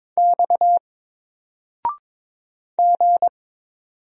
Morse code, heard as a keyed tone, about 23 wpm, 700 Hz. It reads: X G
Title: Single Letter-Number 22wpm